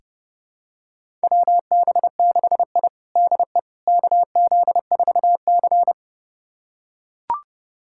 Morse code, heard as a keyed tone, about 32 wpm, 700 Hz. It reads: WB6I DE KZ4C
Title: USA Callsign DE Callsign - 4 Characters - 30wpm